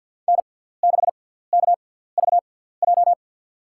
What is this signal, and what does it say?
Morse code, heard as a keyed tone, about 47 wpm, spaced as for 17 wpm, 700 Hz. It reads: N6KVJ
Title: USA Call Signs - 5 Characters - 6x Character Spacing - 50wpm